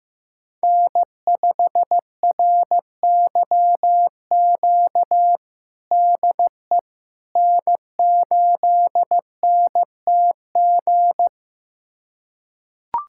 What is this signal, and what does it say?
Morse code, heard as a keyed tone, about 16 wpm, 700 Hz. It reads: N5RYQ DE N8NTG
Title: USA Callsign DE Callsign - 5 Characters - 15wpm